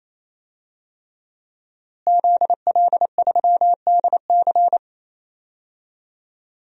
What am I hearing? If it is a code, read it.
Morse code, heard as a keyed tone, about 28 wpm, 690 Hz. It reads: ZL3DC
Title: DXCC-100 Call Signs - 5 Characters - Repeated 3x - 28wpm